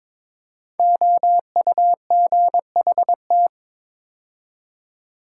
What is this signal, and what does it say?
Morse code, heard as a keyed tone, about 23 wpm, 700 Hz. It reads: OUGHT
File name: 5 Letter Words - First 3 Letters Given - 22wpm